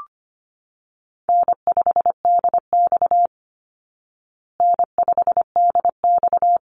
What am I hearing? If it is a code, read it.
Morse code, heard as a keyed tone, about 25 wpm, 700 Hz. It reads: N5DX N5DX
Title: Easy Call Signs Repeated 3x - 25wpm